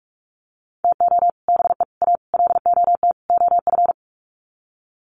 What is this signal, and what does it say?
Morse code, heard as a keyed tone, about 44 wpm, 700 Hz. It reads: TO BE A LOT OF
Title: Sets of 5 Words Encore 45wpm